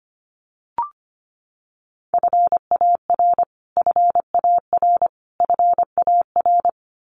Morse code, heard as a keyed tone, about 25 wpm, 700 Hz. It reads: FAR FAR FAR
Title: Top 300 Words Repeated 25wpm